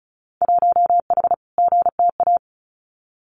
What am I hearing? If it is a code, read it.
Morse code, heard as a keyed tone, about 34 wpm, 700 Hz. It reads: 1H GTA
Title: Winter Field Day Exchanges 35wpm